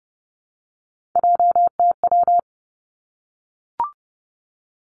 Morse code, heard as a keyed tone, about 30 wpm, 700 Hz. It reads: JTW
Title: Three Challenging Letter-Number Combinations 30wpm